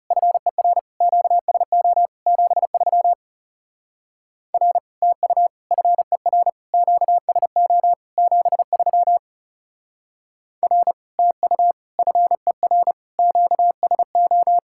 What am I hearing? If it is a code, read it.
Morse code, heard as a keyed tone, about 36 wpm, 700 Hz. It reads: FER QSO 73 R TU FER QSO 73 R TU FER QSO
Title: Speed-Racing Common QSO Phrases - 30wpm